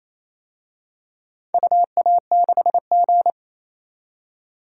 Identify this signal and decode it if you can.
Morse code, heard as a keyed tone, about 28 wpm, 700 Hz. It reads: UA6G
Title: DXCC-100 Call Signs - 4 Characters - 28wpm